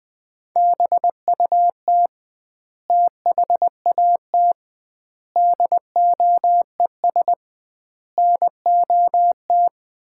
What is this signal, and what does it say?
Morse code, heard as a keyed tone, about 20 wpm, 700 Hz. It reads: BUT THAT DOES NOT
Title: 2x Word-Spacing - Sets of 5 Words - 20wpm